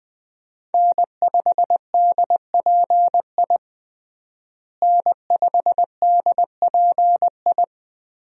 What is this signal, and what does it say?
Morse code, heard as a keyed tone, about 21 wpm, 700 Hz. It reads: N5DPI N5DPI
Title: USA Call Signs - 5 Characters - Repeated 3x - 20wpm